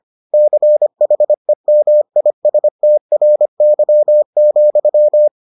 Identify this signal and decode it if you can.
Morse code, heard as a keyed tone, about 25 wpm, 600 Hz. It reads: CHEMISTRY,